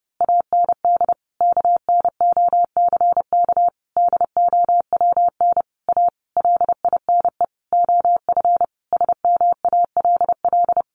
Morse code, heard as a keyed tone, about 30 wpm, 700 Hz. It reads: AND KNOCK DOWN A LINE OF SMALL
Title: A Sound Of Thunder 30wpm